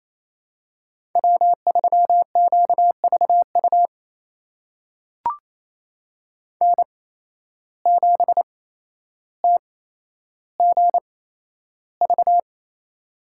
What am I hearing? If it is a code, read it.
Morse code, heard as a keyed tone, about 28 wpm, spaced as for 12 wpm, 700 Hz. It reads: W3QVU N7TGV
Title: USA Call Signs - 5 Characters - 8x Character Spacing - 28wpm